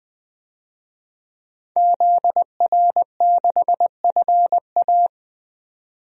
Morse code, heard as a keyed tone, about 20 wpm, 700 Hz. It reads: ZR6FA
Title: DXCC-100 Call Signs - 5 Characters - 20wpm